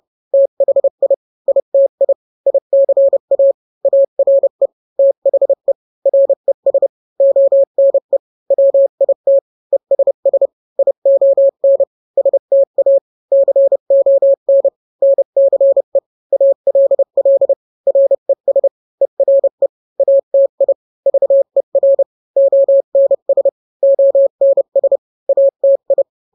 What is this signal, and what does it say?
Morse code, heard as a keyed tone, about 30 wpm, 550 Hz. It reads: THI ITI ICA ARE THE RES ONE WIT ESS ION STA CON NCE ALL RES ERE ATI VER ONS ONS ATI